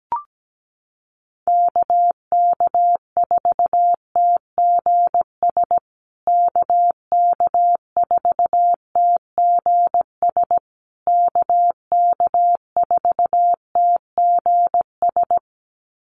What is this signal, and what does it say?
Morse code, heard as a keyed tone, about 17 wpm, 700 Hz. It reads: KK4TGS KK4TGS KK4TGS
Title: USA Call Signs Repeated 3x 17wpm